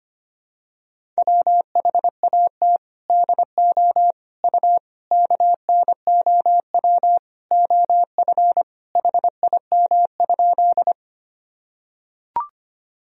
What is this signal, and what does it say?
Morse code, heard as a keyed tone, about 25 wpm, 700 Hz. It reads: WHAT DO U KNOW OF HIM?
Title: Sentences from Top 100 Words and Abbreviations 25wpm